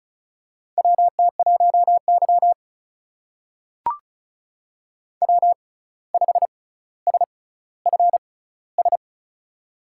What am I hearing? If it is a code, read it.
Morse code, heard as a keyed tone, about 36 wpm, spaced as for 17 wpm, 700 Hz. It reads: WT1Y W5SFS
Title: USA Call Signs - 6x Character Spacing - 35wpm